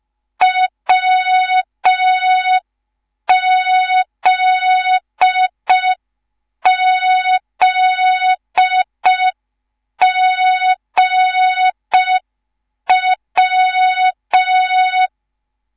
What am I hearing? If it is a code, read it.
Morse code, heard as a keyed tone, about 5 wpm, 3700 Hz. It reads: WZZGW
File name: Morze-HA2MN / hang / 10GWZ HA2MN